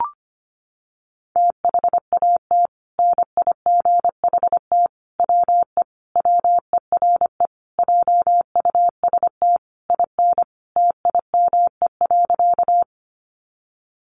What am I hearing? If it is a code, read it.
Morse code, heard as a keyed tone, about 25 wpm, 700 Hz. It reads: THAT NIGHT WE WERE JUST IN TIME.
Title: Sentences from Top 300 Words 25wpm